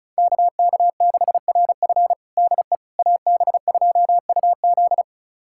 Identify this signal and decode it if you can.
Morse code, heard as a keyed tone, about 36 wpm, 700 Hz. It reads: KK6RF DE AB2UZ
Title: USA Callsign DE Callsign - 5 Characters - 35wpm